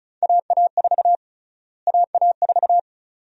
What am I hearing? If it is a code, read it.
Morse code, heard as a keyed tone, about 36 wpm, 700 Hz. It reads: AA4 AA4
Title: USA Call Sign Prefixes - Repeated 2x - 35wpm